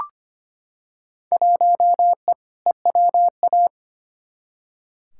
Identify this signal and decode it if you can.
Morse code, heard as a keyed tone, about 25 wpm, 700 Hz. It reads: 1E EWA
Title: Field Day Exchanges - 25wpm